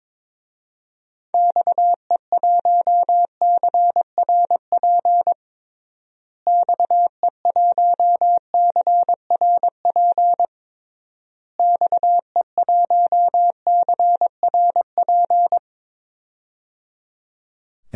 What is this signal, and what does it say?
Morse code, heard as a keyed tone, about 22 wpm, 700 Hz. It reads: XE1CRP XE1CRP XE1CRP
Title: DXCC-100 Call Signs - 6 Characters - Repeated 3x - 22wpm